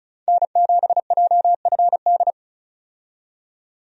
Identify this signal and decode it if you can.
Morse code, heard as a keyed tone, about 36 wpm, 700 Hz. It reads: N7JFD
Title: USA Call Signs - 5 Characters - 35wpm